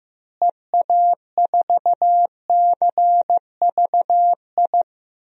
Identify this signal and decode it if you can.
Morse code, heard as a keyed tone, about 15 wpm, 700 Hz. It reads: EA4CVI
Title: DXCC-100 Call Signs - 6 Characters - Repeated 2x - 15wpm